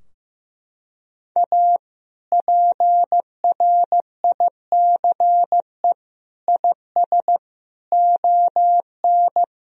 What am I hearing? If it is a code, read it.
Morse code, heard as a keyed tone, about 15 wpm, 700 Hz. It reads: A PRICE IS ON
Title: Sentences from Top 1000 Words - Review Part 4 - 15wpm